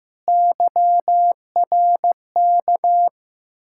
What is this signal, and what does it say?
Morse code, heard as a keyed tone, about 15 wpm, 700 Hz. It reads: YRK
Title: Ontario QSO Party Exchange 15wpm